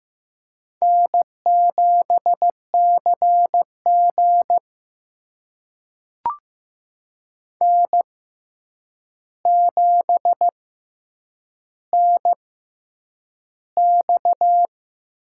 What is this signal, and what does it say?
Morse code, heard as a keyed tone, about 15 wpm, spaced as for 8 wpm, 700 Hz. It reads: N7CG N7NX
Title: USA Call Signs - 6x Character Spacing - 15wpm